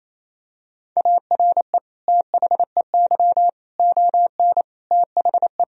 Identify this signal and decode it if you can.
Morse code, heard as a keyed tone, about 28 wpm, 700 Hz. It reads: ARE THEY ON THE
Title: Sentences from Top 1000 Words - Review Part 2 - 28wpm